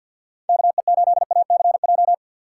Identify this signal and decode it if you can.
Morse code, heard as a keyed tone, about 53 wpm, 700 Hz. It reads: KE8ACJ
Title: USA Call Signs - 6 Characters - 4x Character Spacing - 50wpm